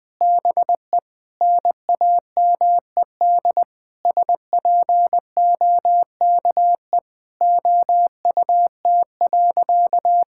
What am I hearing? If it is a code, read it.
Morse code, heard as a keyed tone, about 21 wpm, 700 Hz. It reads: BE NAMED SPOKE OUT.
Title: Sentences with 7-Word Phrases - 20wpm